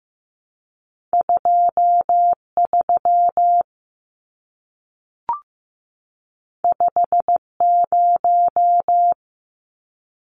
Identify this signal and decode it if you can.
Morse code, heard as a keyed tone, about 15 wpm, 700 Hz. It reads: 23 50
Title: Numbers - 2-Digit 15wpm